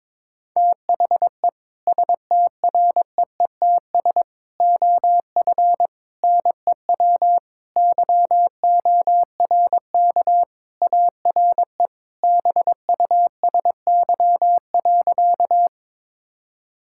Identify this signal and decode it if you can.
Morse code, heard as a keyed tone, about 22 wpm, 700 Hz. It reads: THE STREETS OF NEW YORK ARE BUSY.
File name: practice / 22wpm / Sentences with 3-Word Phrases - 22wpm